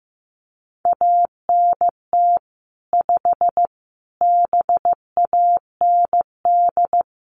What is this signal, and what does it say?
Morse code, heard as a keyed tone, about 15 wpm, 700 Hz. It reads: ANT 5 BAND
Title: Common QSO Phrases 15wpm